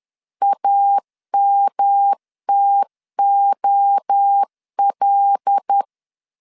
Morse code, heard as a keyed tone, about 10 wpm, 800 Hz. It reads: AMTOL
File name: LEZIONI / 6 PKX0 / 6 PK 10wpm 2436